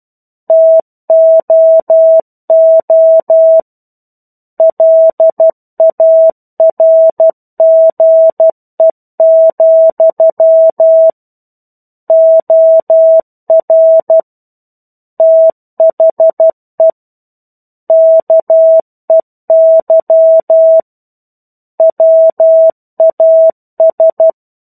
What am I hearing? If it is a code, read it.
Morse code, heard as a keyed tone, about 12 wpm, 650 Hz. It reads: TOO LARGE, OR THE KEY WAS
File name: 01-Chapter-01 12 WPM